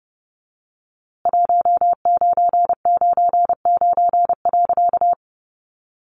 Morse code, heard as a keyed tone, about 30 wpm, 700 Hz. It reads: 1999.
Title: A Sound Of Thunder 30wpm